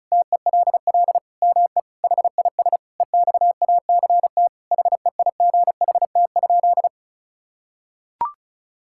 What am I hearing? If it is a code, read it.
Morse code, heard as a keyed tone, about 36 wpm, 700 Hz. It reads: TELL ME HIS EXACT HEIGHT?
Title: Sentences from Top 1600 Words 35wpm